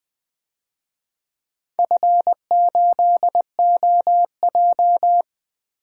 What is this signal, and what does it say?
Morse code, heard as a keyed tone, about 20 wpm, 700 Hz. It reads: F8OJ
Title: DXCC-100 Call Signs - 4 Characters - Repeated 3x - 20wpm